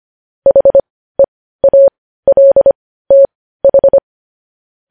Morse code, heard as a keyed tone, about 25 wpm, spaced as for 15 wpm, 550 Hz. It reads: HEALTH
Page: news-headlines-in-morse-code-at-15-wpm